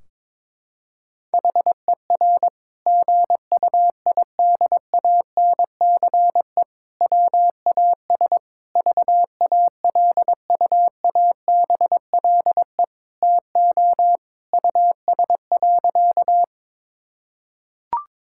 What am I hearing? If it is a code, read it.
Morse code, heard as a keyed tone, about 22 wpm, 700 Hz. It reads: HER GUIDANCE WAS VALUABLE TO US.
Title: Sentences from Top 1600 Words 22wpm